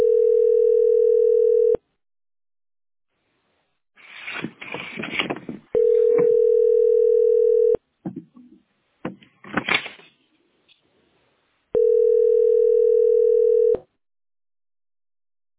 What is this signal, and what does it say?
Ringback tone (North America), 440 + 480 Hz.